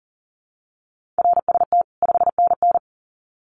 Morse code, heard as a keyed tone, about 40 wpm, 700 Hz. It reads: RST 5NN
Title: Common QSO Phrases 40wpm